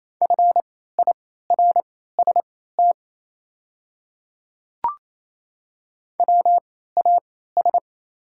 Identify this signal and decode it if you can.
Morse code, heard as a keyed tone, about 28 wpm, spaced as for 16 wpm, 700 Hz. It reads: FIRST WAS
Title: 3x Character Spacing - Top 100 Words - 28wpm